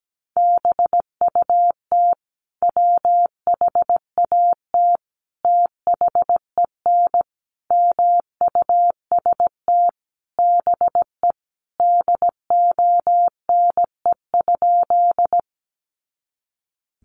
Morse code, heard as a keyed tone, about 17 wpm, 700 Hz. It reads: BUT WHAT THEN MUST BE DONE?
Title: Sentences from Top 500 Words 17wpm